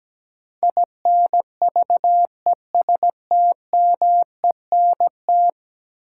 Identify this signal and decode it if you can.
Morse code, heard as a keyed tone, about 17 wpm, 700 Hz. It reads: INVESTMENT
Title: Sentences from Top 700 Words - New Word Drill - 17wpm